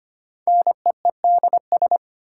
Morse code, heard as a keyed tone, about 25 wpm, 700 Hz. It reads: NEEDS
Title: Sentences from Top 1000 Words - Review Part 3 - New Word Drill - 25wpm